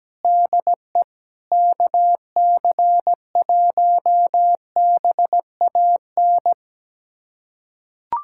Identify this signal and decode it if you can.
Morse code, heard as a keyed tone, about 17 wpm, 700 Hz. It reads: DE KC1BAN
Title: USA Callsign DE Callsign - 6 Characters - 17wpm